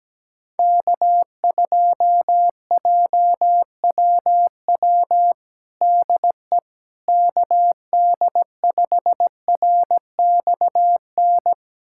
Morse code, heard as a keyed tone, about 17 wpm, 700 Hz. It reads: K2JWW DE KD5RXN
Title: USA Callsign DE Callsign 17wpm